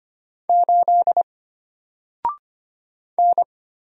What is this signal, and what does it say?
Morse code, heard as a keyed tone, about 25 wpm, 700 Hz. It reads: 8 N
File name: Single-Letter-Number-25wpm